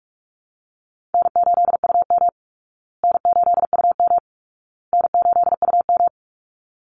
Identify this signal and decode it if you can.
Morse code, heard as a keyed tone, about 44 wpm, 700 Hz. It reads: N8UM N8UM N8UM
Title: Call Signs Repeated 2x - 45wpm